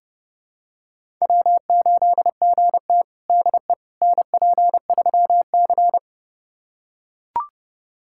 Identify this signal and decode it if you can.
Morse code, heard as a keyed tone, about 30 wpm, 700 Hz. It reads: W8GT DE NP3C
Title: USA Callsign DE Callsign - 4 Characters - 30wpm